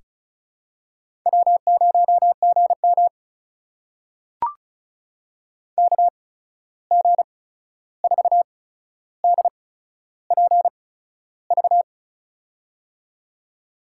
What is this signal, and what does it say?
Morse code, heard as a keyed tone, about 36 wpm, spaced as for 13 wpm, 700 Hz. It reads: W0GM KG4DPV
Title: USA Call Signs - 8x Character Spacing - 35wpm